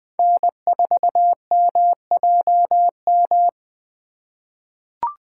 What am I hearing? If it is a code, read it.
Morse code, heard as a keyed tone, about 20 wpm, 700 Hz. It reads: N4MJM
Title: USA Call Signs - 5 Characters - 3x Character Spacing - 20wpm